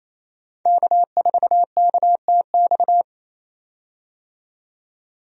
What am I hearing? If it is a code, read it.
Morse code, heard as a keyed tone, about 28 wpm, 700 Hz. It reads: K4KTX
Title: USA Call Signs - 5 Characters - 28wpm